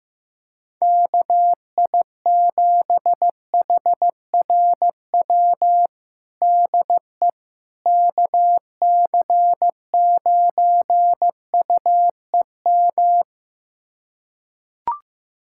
Morse code, heard as a keyed tone, about 16 wpm, 700 Hz. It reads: KI7HRW DE KC9UEM
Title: USA Callsign DE Callsign - 6 Characters - 15wpm